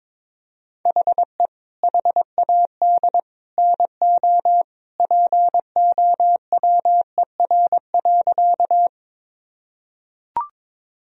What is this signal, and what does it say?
Morse code, heard as a keyed tone, about 22 wpm, 700 Hz. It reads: HE HAD NO POWER.
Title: Sentences from Top 1000 Words - Review Part 2 - 22wpm